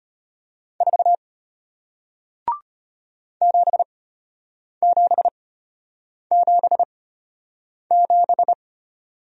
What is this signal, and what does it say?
Morse code, heard as a keyed tone, about 33 wpm, spaced as for 8 wpm, 700 Hz. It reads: 4 7777